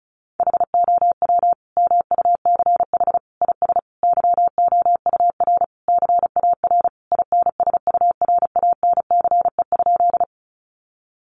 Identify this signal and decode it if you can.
Morse code, heard as a keyed tone, about 34 wpm, 700 Hz. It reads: HOW MUCH IS YOUR CAR INSURANCE?
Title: Sentences from Top 1100 Words 35wpm